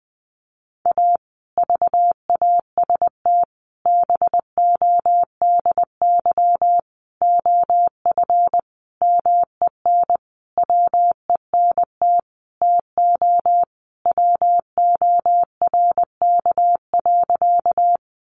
Morse code, heard as a keyed tone, about 20 wpm, 700 Hz. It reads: A VAST BODY OF MEN WENT TO WORK.
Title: Sentences from Top 1300 Words 20wpm